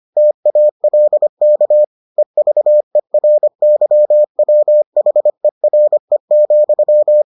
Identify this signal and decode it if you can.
Morse code, heard as a keyed tone, about 25 wpm, 600 Hz. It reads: TALK EVERYWHERE,